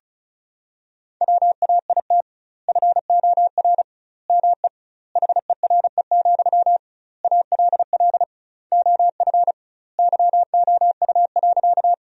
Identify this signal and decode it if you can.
Morse code, heard as a keyed tone, about 36 wpm, 700 Hz. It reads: WAIT FOR ME HERE, ALL OF YOU.
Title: 2x Word-Spacing - Sentences from Top 500 Words - 35wpm